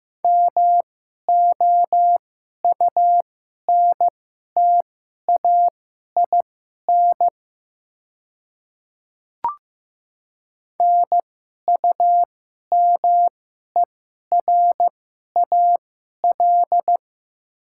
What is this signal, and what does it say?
Morse code, heard as a keyed tone, about 15 wpm, spaced as for 11 wpm, 700 Hz. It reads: MOUNTAIN NUMERAL
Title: 2x Character Spacing - Top 500 Words - 15wpm